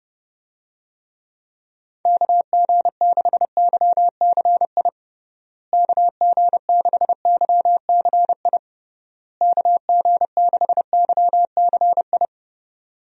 Morse code, heard as a keyed tone, about 32 wpm, 700 Hz. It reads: KG6YCI KG6YCI KG6YCI
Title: USA Call Signs - 6 Characters - Repeated 3x - 30wpm